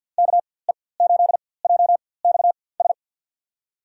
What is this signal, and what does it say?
Morse code, heard as a keyed tone, about 52 wpm, spaced as for 23 wpm, 700 Hz. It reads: KE8JXS
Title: USA Call Signs - 6 Characters - 4x Character Spacing - 50wpm